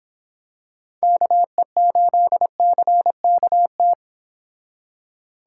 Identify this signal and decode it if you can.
Morse code, heard as a keyed tone, about 26 wpm, 700 Hz. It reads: KE8CKT
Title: Speed Racing - USA Call Signs 22wpm